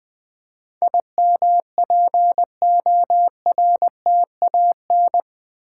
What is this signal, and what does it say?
Morse code, heard as a keyed tone, about 20 wpm, 700 Hz. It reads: IMPORTAN
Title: Build A Word - 4 Letters At A Time - 10-Letter Words - 20wpm